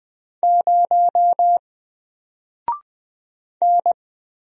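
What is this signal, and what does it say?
Morse code, heard as a keyed tone, about 20 wpm, 700 Hz. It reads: 0 N
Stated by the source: Single letter and number Morse drill at 20 wpm, from N4LCD.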